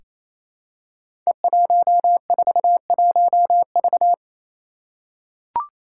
Morse code, heard as a keyed tone, about 28 wpm, 700 Hz. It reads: E141V
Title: Five Challenging Letter-Number Combinations 28wpm